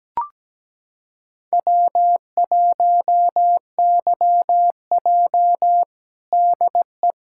Morse code, heard as a keyed tone, about 17 wpm, 700 Hz. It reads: W1YJ DE
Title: USA Callsign DE Callsign - 4 Characters - 17wpm